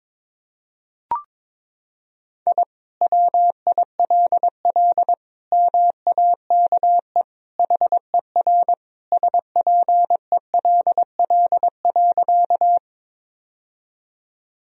Morse code, heard as a keyed tone, about 22 wpm, 700 Hz. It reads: I WILL MAKE HER SPELL.
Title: Sentences from Top 1000 Words - Review Part 1 - 22wpm